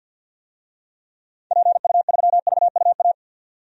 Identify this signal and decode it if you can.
Morse code, heard as a keyed tone, about 46 wpm, 700 Hz. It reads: PU3VUA
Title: DXCC-100 Call Signs - 6 Characters - 50wpm